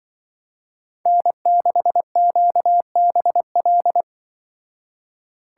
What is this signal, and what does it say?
Morse code, heard as a keyed tone, about 24 wpm, 700 Hz. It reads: N6QBL
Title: Speed Racing - USA 1x3 Call Signs 20wpm